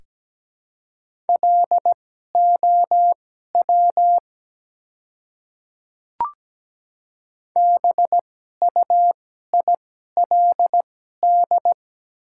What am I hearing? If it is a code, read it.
Morse code, heard as a keyed tone, about 17 wpm, spaced as for 12 wpm, 700 Hz. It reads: LOW BUILD
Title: 2x Character Spacing - Top 300 Words - 17wpm